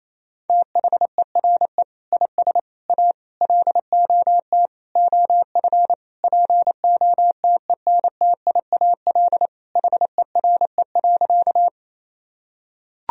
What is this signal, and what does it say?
Morse code, heard as a keyed tone, about 28 wpm, 700 Hz. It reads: THERE IS A LOT OF POTENTIAL HERE.
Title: Sentences from Top 1000 Words - Review Part 4 - 28wpm